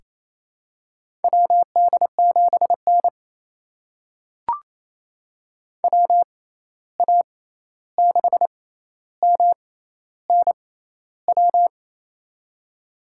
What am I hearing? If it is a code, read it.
Morse code, heard as a keyed tone, about 28 wpm, spaced as for 13 wpm, 700 Hz. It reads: WD7N WA6MNW
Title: USA Call Signs - 6x Character Spacing - 28wpm